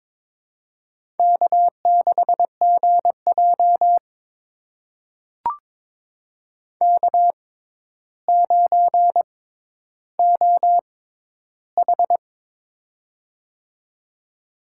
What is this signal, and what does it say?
Morse code, heard as a keyed tone, about 22 wpm, spaced as for 11 wpm, 700 Hz. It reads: K6GJ K9OH
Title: USA Call Signs - 4 Characters - 6x Character Spacing - 22wpm